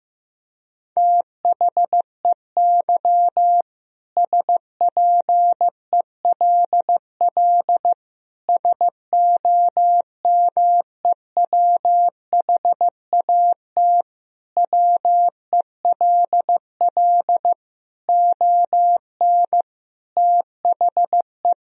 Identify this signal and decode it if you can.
Morse code, heard as a keyed tone, about 15 wpm, 700 Hz. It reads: THEY SPELL SOMEWHAT WELL ON THE
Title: Sentences from Top 1000 Words - Review Part 1 - 15wpm